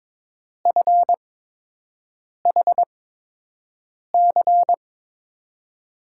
Morse code, heard as a keyed tone, about 22 wpm, spaced as for 6 wpm, 700 Hz. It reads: FHC
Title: USA Call Signs - 5 Characters - 8x Character Spacing - 22wpm